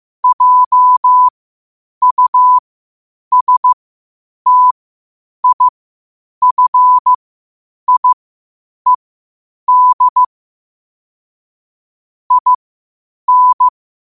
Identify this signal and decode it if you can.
Morse code, heard as a keyed tone, about 15 wpm, spaced as for 8 wpm, 1000 Hz. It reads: JUSTIFIED IN
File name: qotdfiles / 8wpm